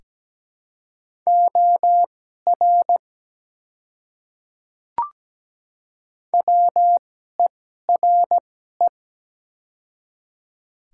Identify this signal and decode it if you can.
Morse code, heard as a keyed tone, about 17 wpm, spaced as for 12 wpm, 700 Hz. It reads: OR WERE
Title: 2x Character Spacing - Top 100 Words - 17wpm